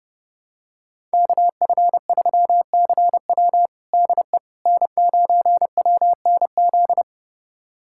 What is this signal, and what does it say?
Morse code, heard as a keyed tone, about 29 wpm, 700 Hz. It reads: KF3CW DE N9WNZ
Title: USA Callsign DE Callsign - 5 Characters - 30wpm